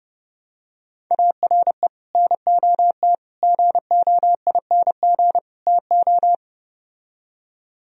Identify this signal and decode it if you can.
Morse code, heard as a keyed tone, about 32 wpm, 700 Hz. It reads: ARE NOT GOING TO